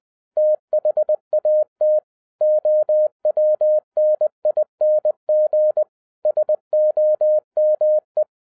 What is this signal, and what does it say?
Morse code, heard as a keyed tone, about 20 wpm, 610 Hz. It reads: THAT OWNING SOME
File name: 2011-07-mp3 / n58 BACK FROM A BIG ACCIDENT 20wpm